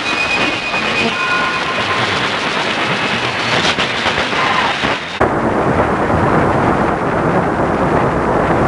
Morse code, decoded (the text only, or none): none